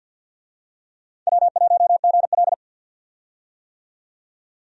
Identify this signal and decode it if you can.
Morse code, heard as a keyed tone, about 53 wpm, 700 Hz. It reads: W1GL
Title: USA Call Signs - 4 Characters - 50wpm